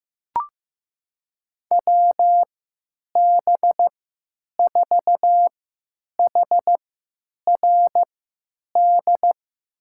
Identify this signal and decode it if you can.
Morse code, heard as a keyed tone, about 15 wpm, spaced as for 9 wpm, 700 Hz. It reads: WB4HRD
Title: USA Call Signs - 6 Characters - 3x Character Spacing - 15wpm